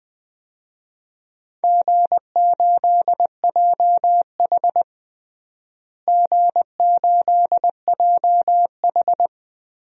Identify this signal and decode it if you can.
Morse code, heard as a keyed tone, about 20 wpm, 700 Hz. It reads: G8JH G8JH
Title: DXCC-100 Call Signs - 4 Characters - Repeated 3x - 20wpm